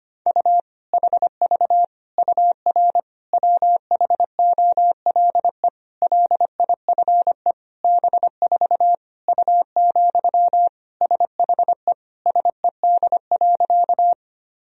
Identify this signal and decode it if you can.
Morse code, heard as a keyed tone, about 25 wpm, 700 Hz. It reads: U HV UR WHOLE LIFE B4 U, SHE SED.
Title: Sentences from Top 400 Words and Abbreviations 25wpm